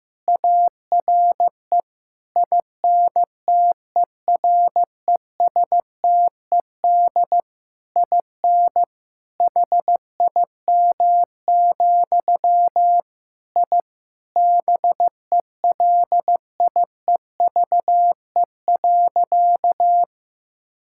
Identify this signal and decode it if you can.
Morse code, heard as a keyed tone, about 15 wpm, 700 Hz. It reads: ARE INTERESTED IN HIM, I BELIEVE.
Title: Sentences from Top 1000 Words - Review Part 6 - 15wpm